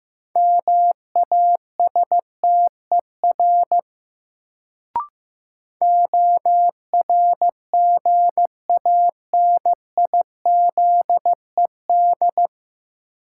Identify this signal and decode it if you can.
Morse code, heard as a keyed tone, about 15 wpm, 700 Hz. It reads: MASTER ORGANIZED
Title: Sentences from Top 1300 Words - New Word Drill - 15wpm